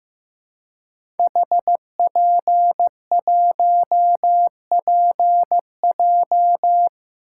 Morse code, heard as a keyed tone, about 15 wpm, 700 Hz. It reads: HP1PJ
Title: Code Groups - 5 Letter:Numbers - 15wpm